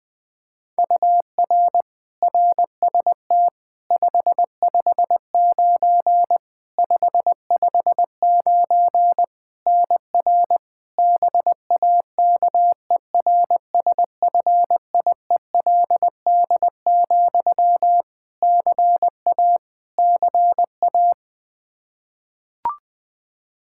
Morse code, heard as a keyed tone, about 20 wpm, 700 Hz. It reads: UR RST 559 559 NR BAKERSFIELD, CA CA
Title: Practice QSO - Exchange 1 - Beginning - 20wpm